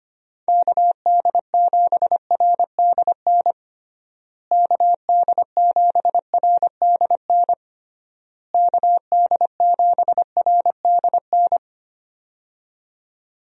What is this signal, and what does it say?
Morse code, heard as a keyed tone, about 25 wpm, 700 Hz. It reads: KD7RDN KD7RDN KD7RDN
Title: USA Call Signs - 6 Characters - Repeated 3x - 25wpm